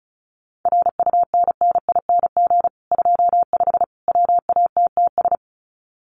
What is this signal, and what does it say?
Morse code, heard as a keyed tone, about 34 wpm, 700 Hz. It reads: RUNNING 25 WATTS
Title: Common QSO Phrases 35wpm